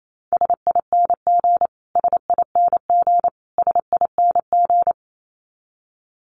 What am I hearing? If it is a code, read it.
Morse code, heard as a keyed tone, about 28 wpm, 700 Hz. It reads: SING SING SING
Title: Top 500 Words Repeated 28wpm